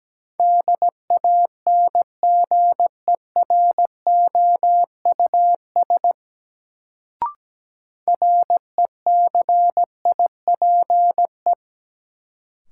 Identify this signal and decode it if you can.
Morse code, heard as a keyed tone, about 17 wpm, 700 Hz. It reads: DANGEROUS RECIPE
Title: Sentences from Top 1200 Words - New Word Drill Alt - 17wpm